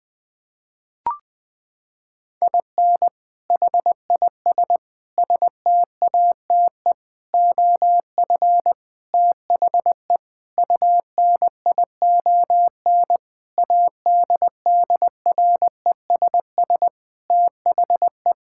Morse code, heard as a keyed tone, about 20 wpm, 700 Hz. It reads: IN HIS STATE OF THE UNION ADDRESS THE
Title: Sentences with 7-Word Phrases - 20wpm